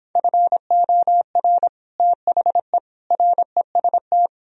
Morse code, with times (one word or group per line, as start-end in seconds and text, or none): 0.2-1.7 FOR
2.0-2.8 THE
3.1-4.3 REST